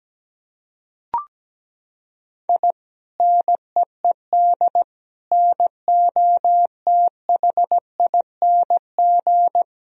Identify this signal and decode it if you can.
Morse code, heard as a keyed tone, about 17 wpm, 700 Hz. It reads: I NEED NOTHING
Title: Sentences from Top 1000 Words - Review Part 1 - 17wpm